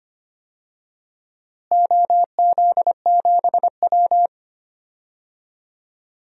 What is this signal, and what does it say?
Morse code, heard as a keyed tone, about 25 wpm, 700 Hz. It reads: OZ7W
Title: DXCC-100 Call Signs - 4 Characters - Repeated 2x - 25wpm